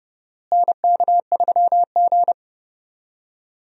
Morse code, heard as a keyed tone, about 30 wpm, 700 Hz. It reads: NK3G
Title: USA Call Signs - 4 Characters - Repeated 2x - 30wpm